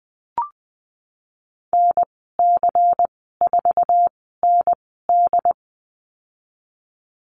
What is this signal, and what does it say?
Morse code, heard as a keyed tone, about 20 wpm, spaced as for 15 wpm, 700 Hz. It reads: NC4ND
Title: 167 Lesson 27g - Number 7 - Call Signs 20wpm